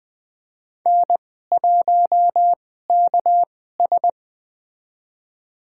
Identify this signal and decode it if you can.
Morse code, heard as a keyed tone, about 20 wpm, spaced as for 14 wpm, 700 Hz. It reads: N1KS
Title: USA Call Signs - 4 Characters - 2x Character Spacing - 20wpm